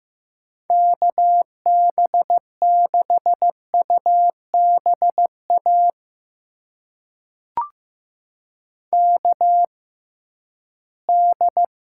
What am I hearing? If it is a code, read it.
Morse code, heard as a keyed tone, about 15 wpm, spaced as for 11 wpm, 700 Hz. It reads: KB6UBA KD
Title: USA Call Signs - 6x Character Spacing - 15wpm